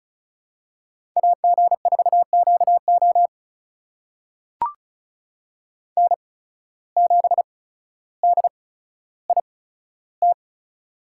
Morse code, heard as a keyed tone, about 34 wpm, spaced as for 15 wpm, 700 Hz. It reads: AG4QO N7DIT
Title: USA Call Signs - 5 Characters - 8x Character Spacing - 35wpm